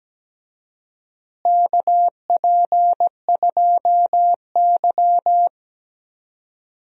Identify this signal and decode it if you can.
Morse code, heard as a keyed tone, about 17 wpm, 700 Hz. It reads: KP2Y
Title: DXCC-100 Call Signs - 4 Characters - Repeated 2x - 17wpm